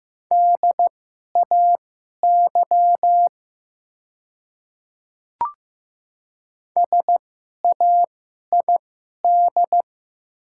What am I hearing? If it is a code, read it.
Morse code, heard as a keyed tone, about 15 wpm, spaced as for 11 wpm, 690 Hz. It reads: DAY SAID
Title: 2x Character Spacing - Top 200 Words - 15wpm